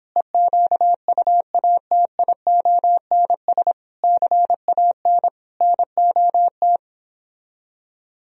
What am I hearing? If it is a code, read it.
Morse code, heard as a keyed tone, about 26 wpm, 700 Hz. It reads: EQUATIONS CAN NOT
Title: Speed-Racing - Sets of 5 Words 22wpm